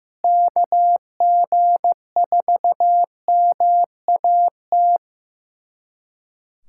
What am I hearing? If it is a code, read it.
Morse code, heard as a keyed tone, about 15 wpm, 700 Hz. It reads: KG4MAT
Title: USA Callsign DE Callsign 15wpm